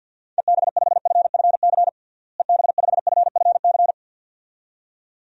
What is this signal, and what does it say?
Morse code, heard as a keyed tone, about 51 wpm, 700 Hz. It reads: EB5FFC EB5FFC
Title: DXCC-100 Call Signs - 6 Characters - Repeated 2x - 50wpm